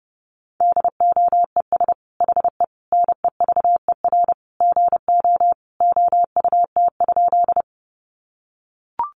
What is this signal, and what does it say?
Morse code, heard as a keyed tone, about 28 wpm, 700 Hz. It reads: DOES HE NEVER GO OUT?
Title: Sentences from Top 300 Words 30wpm